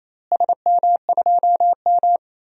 Morse code, heard as a keyed tone, about 27 wpm, 700 Hz. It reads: SM2M
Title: DXCC-100 Call Signs - 4 Characters - Repeated 2x - 28wpm